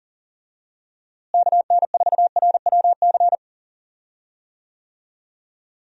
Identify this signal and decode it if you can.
Morse code, heard as a keyed tone, about 40 wpm, 700 Hz. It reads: KN4RWC
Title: USA Call Signs - 6 Characters - Repeated 2x - 40wpm